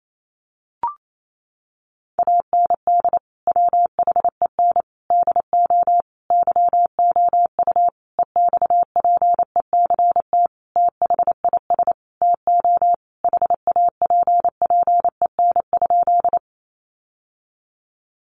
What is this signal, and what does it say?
Morse code, heard as a keyed tone, about 28 wpm, 700 Hz. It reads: AND WHEN DO YOU EXPECT THIS TO HAPPEN?
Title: Sentences from Top 1100 Words 28wpm